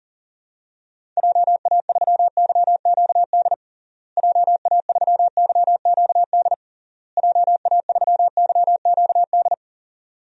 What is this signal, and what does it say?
Morse code, heard as a keyed tone, about 40 wpm, 700 Hz. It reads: JA3YQD JA3YQD JA3YQD
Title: DXCC-100 Call Signs - 6 Characters - Repeated 3x - 40wpm